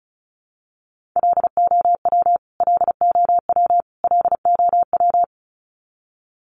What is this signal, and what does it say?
Morse code, heard as a keyed tone, about 34 wpm, 700 Hz. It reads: LOW LOW LOW
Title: Top 500 Words Repeated 35wpm